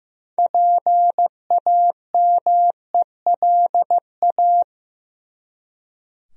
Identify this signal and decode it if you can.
Morse code, heard as a keyed tone, about 15 wpm, 700 Hz. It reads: PAMELA
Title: USA Common Names - 15wpm